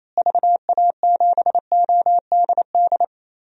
Morse code, heard as a keyed tone, about 28 wpm, 700 Hz. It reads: VA7ODD
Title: DXCC-100 Call Signs - 6 Characters - Repeated 3x - 28wpm